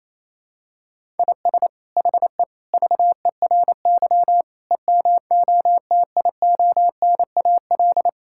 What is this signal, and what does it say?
Morse code, heard as a keyed tone, about 28 wpm, 700 Hz. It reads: IS HE VERY EMOTIONAL
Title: Sentences from Top 1000 Words - Review Part 5 - 28wpm